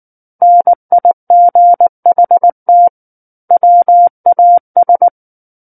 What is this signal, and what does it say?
Morse code, heard as a keyed tone, about 19 wpm, 700 Hz. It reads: NIGHT WAS
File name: Book One / 10-Chapter 10 - 19 WPM